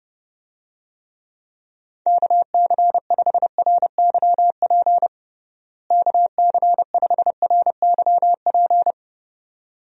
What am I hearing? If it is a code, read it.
Morse code, heard as a keyed tone, about 30 wpm, 700 Hz. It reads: KC5RYP KC5RYP
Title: USA Call Signs - 6 Characters - Repeated 2x - 30wpm